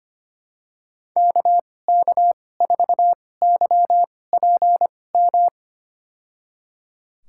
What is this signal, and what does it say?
Morse code, heard as a keyed tone, about 25 wpm, spaced as for 18 wpm, 700 Hz. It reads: KK4YPM
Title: USA Call Signs - 6 Characters - 2x Character Spacing - 25wpm